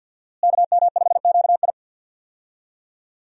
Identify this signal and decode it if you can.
Morse code, heard as a keyed tone, about 52 wpm, 700 Hz. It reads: KM5QI
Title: USA Call Signs - 5 Characters - Repeated 3x - 50wpm